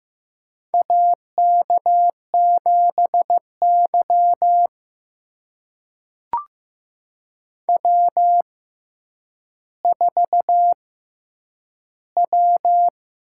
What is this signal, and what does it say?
Morse code, heard as a keyed tone, about 15 wpm, spaced as for 8 wpm, 700 Hz. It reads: AK7Y W4W
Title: USA Call Signs - 4 Characters - 6x Character Spacing - 15wpm